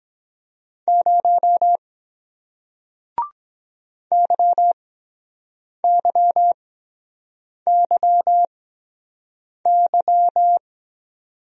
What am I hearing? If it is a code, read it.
Morse code, heard as a keyed tone, about 23 wpm, spaced as for 7 wpm, 700 Hz. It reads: 0 YYYY